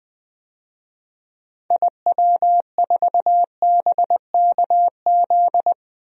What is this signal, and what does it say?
Morse code, heard as a keyed tone, about 20 wpm, 700 Hz. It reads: IW4BKZ
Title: DXCC-100 Call Signs - 6 Characters - 20wpm